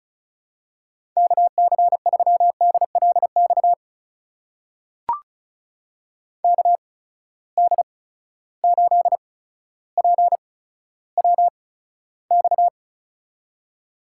Morse code, heard as a keyed tone, about 36 wpm, spaced as for 15 wpm, 700 Hz. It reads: KC3DLX KD8PWX
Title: USA Call Signs - 6 Characters - 8x Character Spacing - 35wpm